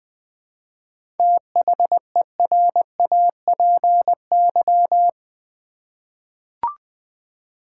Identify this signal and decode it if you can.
Morse code, heard as a keyed tone, about 20 wpm, 700 Hz. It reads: THERAPY
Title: 7 Letter Words - First 3 Letters Given - 20wpm